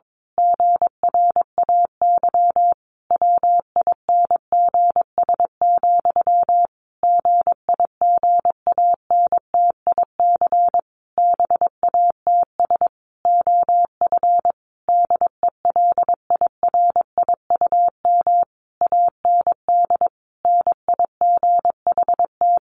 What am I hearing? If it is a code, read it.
Morse code, heard as a keyed tone, about 22 wpm, 700 Hz. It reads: GRAY WINGS, GIGANTIC BATS OF DELIRIUM AND NIGHT
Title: A Sound Of Thunder 22wpm